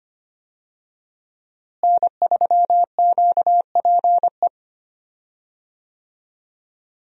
Morse code, heard as a keyed tone, about 25 wpm, 700 Hz. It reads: N3QPE
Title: USA Call Signs - 5 Characters - Repeated 2x - 25wpm